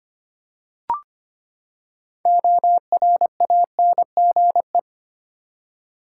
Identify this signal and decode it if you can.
Morse code, heard as a keyed tone, about 25 wpm, 700 Hz. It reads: ORANGE
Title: Six Letter Words - 25wpm